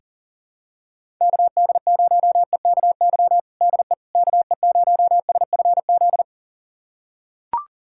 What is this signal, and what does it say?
Morse code, heard as a keyed tone, about 40 wpm, 700 Hz. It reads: KD0EKY DE KE0SFZ
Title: USA Callsign DE Callsign - 6 Characters - 40wpm